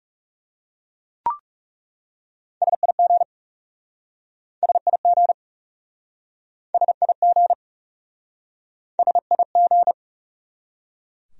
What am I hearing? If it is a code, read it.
Morse code, heard as a keyed tone, about 40 wpm, 700 Hz. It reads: SIG SIG SIG SIG